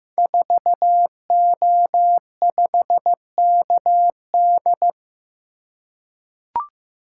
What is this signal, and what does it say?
Morse code, heard as a keyed tone, about 15 wpm, 700 Hz. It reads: 4O5KD
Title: Code Groups - 5 Letter:Numbers - 15wpm